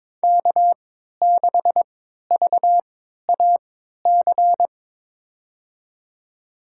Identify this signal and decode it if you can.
Morse code, heard as a keyed tone, about 22 wpm, spaced as for 13 wpm, 700 Hz. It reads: K6VAC
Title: USA Call Signs - 5 Characters - 3x Character Spacing - 22wpm